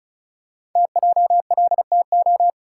Morse code, heard as a keyed tone, about 36 wpm, 700 Hz. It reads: TJLTO
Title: Code Groups - 5 Letters - 35wpm